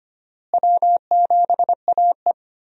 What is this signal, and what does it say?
Morse code, heard as a keyed tone, about 25 wpm, 700 Hz. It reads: W7AE
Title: USA Call Signs - 4 Characters - Repeated 2x - 25wpm